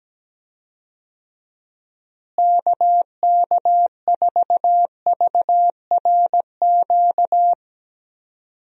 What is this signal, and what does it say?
Morse code, heard as a keyed tone, about 17 wpm, 700 Hz. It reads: KK4VRQ
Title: USA Call Signs - 6 Characters - Repeated 3x - 17wpm